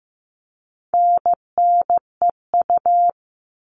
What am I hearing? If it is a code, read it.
Morse code, heard as a keyed tone, about 15 wpm, 700 Hz. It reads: NNEU
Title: All Cut Numbers - 4-Digit 15wpm